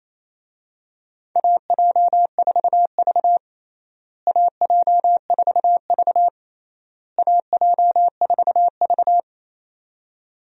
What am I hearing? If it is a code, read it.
Morse code, heard as a keyed tone, about 27 wpm, 700 Hz. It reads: AJ4V AJ4V AJ4V
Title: USA Call Signs - 4 Characters - Repeated 3x - 28wpm